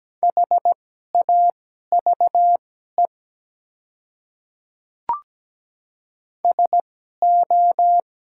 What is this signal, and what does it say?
Morse code, heard as a keyed tone, about 17 wpm, spaced as for 12 wpm, 700 Hz. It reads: HAVE SO
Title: 2x Character Spacing - Top 100 Words - 17wpm